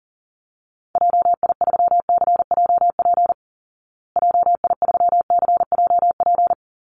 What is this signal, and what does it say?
Morse code, heard as a keyed tone, about 40 wpm, 700 Hz. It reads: JI3CJP JI3CJP
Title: Call Signs Repeated 2x - 40wpm